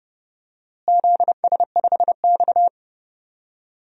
Morse code, heard as a keyed tone, about 30 wpm, 700 Hz. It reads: ZS5X
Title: DXCC-100 Call Signs - 4 Characters - Repeated 3x - 30wpm